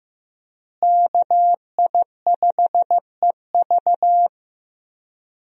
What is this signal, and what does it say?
Morse code, heard as a keyed tone, about 15 wpm, 700 Hz. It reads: KI5EV
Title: USA Call Signs - 5 Characters - Repeated 3x - 15wpm